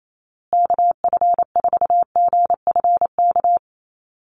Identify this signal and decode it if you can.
Morse code, heard as a keyed tone, about 28 wpm, 700 Hz. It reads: KF4GFK
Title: USA Call Signs Repeated 2x 28wpm